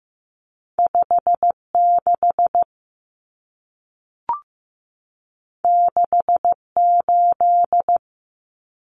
Morse code, heard as a keyed tone, about 15 wpm, 700 Hz. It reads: 56 68
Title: Numbers - 2-Digit 15wpm